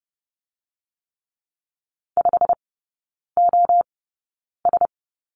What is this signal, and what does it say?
Morse code, heard as a keyed tone, about 30 wpm, spaced as for 9 wpm, 700 Hz. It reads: 5OS
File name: Lesson 17e - Letter u - Three Characters 30wpm